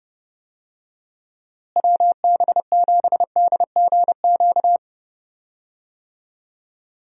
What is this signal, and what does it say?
Morse code, heard as a keyed tone, about 30 wpm, 700 Hz. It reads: WB7DGQ
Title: USA Call Signs - 6 Characters - Repeated 2x - 30wpm